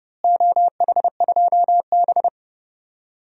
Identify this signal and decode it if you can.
Morse code, heard as a keyed tone, about 32 wpm, 700 Hz. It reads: OH2B
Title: DXCC-100 Call Signs - 4 Characters - Repeated 3x - 30wpm